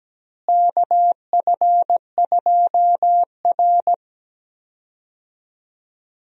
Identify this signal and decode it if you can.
Morse code, heard as a keyed tone, about 17 wpm, 700 Hz. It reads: KF2R
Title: USA Call Signs - 4 Characters - Repeated 2x - 17wpm